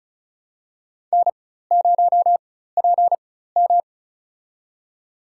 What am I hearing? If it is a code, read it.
Morse code, heard as a keyed tone, about 36 wpm, spaced as for 16 wpm, 700 Hz. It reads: N0PM
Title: USA Call Signs - 4 Characters - 4x Character Spacing - 35wpm